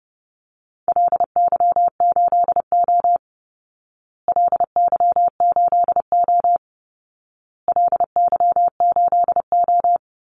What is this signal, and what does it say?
Morse code, heard as a keyed tone, about 30 wpm, 700 Hz. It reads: LY8O LY8O LY8O
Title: Call Signs Repeated 2x - 30wpm